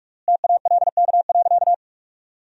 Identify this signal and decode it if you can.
Morse code, heard as a keyed tone, about 46 wpm, 700 Hz. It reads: TALK.
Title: 3x Word-Spacing - Sentences from Top 400 Words - 45wpm